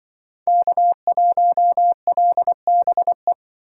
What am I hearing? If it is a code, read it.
Morse code, heard as a keyed tone, about 24 wpm, 700 Hz. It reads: K1LBE
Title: Speed Racing - USA Call Signs 20wpm